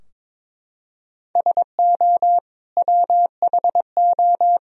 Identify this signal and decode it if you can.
Morse code, heard as a keyed tone, about 21 wpm, 700 Hz. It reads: SO WHO
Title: Sentences from Top 1000 Words - Review Part 3 - 22wpm